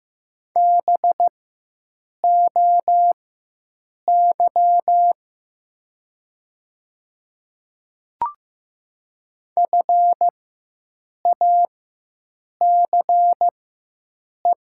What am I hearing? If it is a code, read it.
Morse code, heard as a keyed tone, about 15 wpm, spaced as for 7 wpm, 700 Hz. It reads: BOY FACE
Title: 4x Character Spacing - Top 500 Words - 15wpm